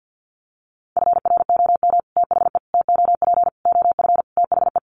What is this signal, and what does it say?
Morse code, heard as a keyed tone, about 50 wpm, 700 Hz. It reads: FROM THE TOP OF THE
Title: Sets of 5 Words Encore 50wpm